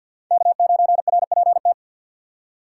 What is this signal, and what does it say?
Morse code, heard as a keyed tone, about 47 wpm, 690 Hz. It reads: K9RPT
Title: USA Call Signs - 5 Characters - 2x Character Spacing - 50wpm